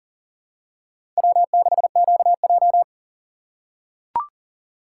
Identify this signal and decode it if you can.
Morse code, heard as a keyed tone, about 40 wpm, 700 Hz. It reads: W6QJ
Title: USA Call Signs - 4x Character Spacing - 40wpm